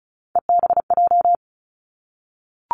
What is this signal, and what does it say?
Morse code, heard as a keyed tone, about 36 wpm, 700 Hz. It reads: EBJ
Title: Three Challenging Letter-Number Combinations 35wpm